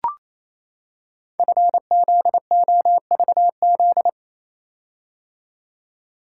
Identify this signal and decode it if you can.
Morse code, heard as a keyed tone, about 27 wpm, 700 Hz. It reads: FZOVZ
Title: Code Groups - 5 Letters - 28wpm